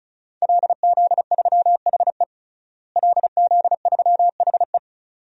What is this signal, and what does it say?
Morse code, heard as a keyed tone, about 35 wpm, 700 Hz. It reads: LZ3HE LZ3HE
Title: DXCC-100 Call Signs - 5 Characters - Repeated 2x - 35wpm